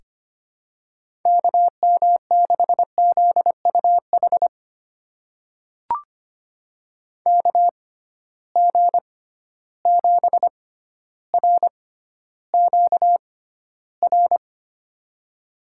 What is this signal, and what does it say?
Morse code, heard as a keyed tone, about 25 wpm, spaced as for 13 wpm, 700 Hz. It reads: KM6ZUH KG7RQR
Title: USA Call Signs - 6 Characters - 6x Character Spacing - 25wpm